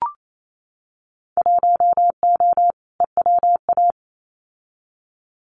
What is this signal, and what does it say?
Morse code, heard as a keyed tone, about 28 wpm, 700 Hz. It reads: 1O EWA
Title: Winter Field Day Exchanges Encore 28wpm